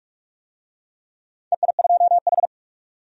Morse code, heard as a keyed tone, about 46 wpm, 700 Hz. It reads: EI2H
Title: DXCC-100 Call Signs - 4 Characters - 45wpm